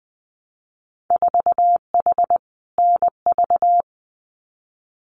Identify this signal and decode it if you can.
Morse code, heard as a keyed tone, about 20 wpm, 700 Hz. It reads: 4H NV
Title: Winter Field Day Exchanges Encore 20wpm